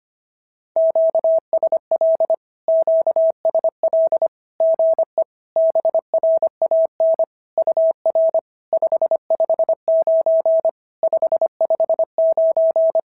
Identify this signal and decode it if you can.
Morse code, heard as a keyed tone, about 24 wpm, 650 Hz. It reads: QSL QSL GE BRAN UR 559 559
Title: SOTA QSO Practice 25wpm